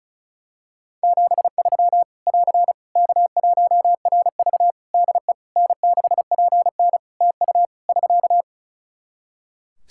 Morse code, heard as a keyed tone, about 36 wpm, 700 Hz. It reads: 73 <AR> K1RV DE N6PN TU <SK>
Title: Practice QSO - Exchange 4 - Ending - 35wpm